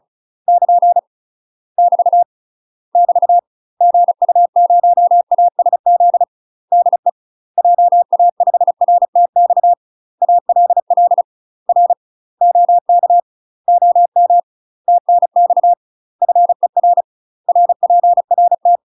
Morse code, heard as a keyed tone, about 34 wpm, 700 Hz. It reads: <KN> <BT> <BT> GU0ASZ DE JA5RTX ALL R OK OM TNX FER RPRT
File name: UK MASTER QSOs TRAINING / 35WPM / UK MASTER QSOs 0026